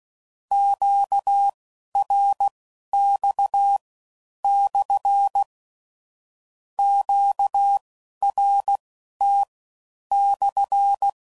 Morse code, heard as a keyed tone, about 16 wpm, spaced as for 11 wpm, 800 Hz. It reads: QRX/ QRT/
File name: CW Code Groups @15WPM/Q Calls